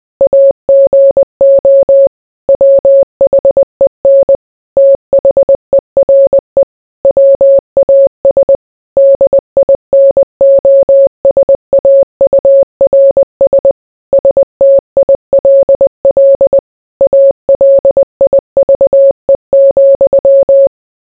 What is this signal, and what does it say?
Morse code, heard as a keyed tone, about 20 wpm, 550 Hz. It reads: AGO WHEN THERE WAS DINOSAURS STILL ALIVE,